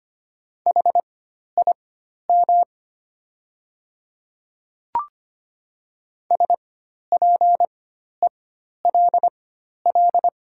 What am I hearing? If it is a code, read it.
Morse code, heard as a keyed tone, about 25 wpm, spaced as for 12 wpm, 700 Hz. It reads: HIM SPELL
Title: 4x Character Spacing - Top 200 Words - 25wpm